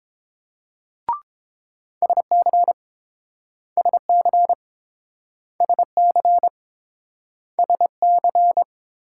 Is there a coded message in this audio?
Yes: Morse code from 2.0 to 8.6 s, heard as a keyed tone, about 30 wpm, 700 Hz. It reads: SC SC SC SC